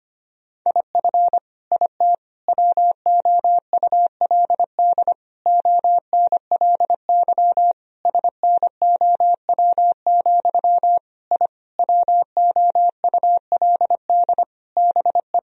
Morse code, heard as a keyed tone, about 25 wpm, 700 Hz. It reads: IF IT WOULD ONLY SNOW, I WOULD BE
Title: Sentences from Top 1000 Words - Review Part 3 - 25wpm